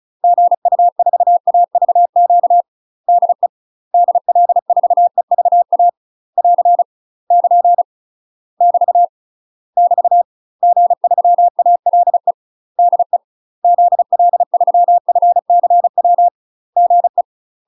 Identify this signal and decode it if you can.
Morse code, heard as a keyed tone, about 36 wpm, 700 Hz. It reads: GU4AVQ DE DL4EVA <AR> <KN> <BT> <BT> G3ALE DE ZL3FCW GE
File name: UK MASTER QSOs TRAINING / 35WPM / UK MASTER QSOs 0017